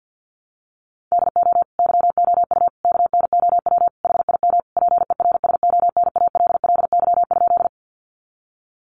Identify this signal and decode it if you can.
Morse code, heard as a keyed tone, about 50 wpm, 700 Hz. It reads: DO YOU KNOW HIM PERSONALLY?
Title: Sentences from Top 1400 Words 50wpm